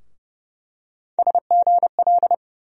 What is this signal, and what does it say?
Morse code, heard as a keyed tone, about 30 wpm, 700 Hz. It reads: SGL
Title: USA Call Sign Postfixes - 3 Letters 30wpm